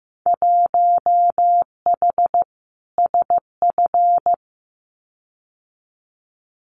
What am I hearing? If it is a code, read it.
Morse code, heard as a keyed tone, about 15 wpm, 700 Hz. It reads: 1H SF
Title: Winter Field Day Exchanges 15wpm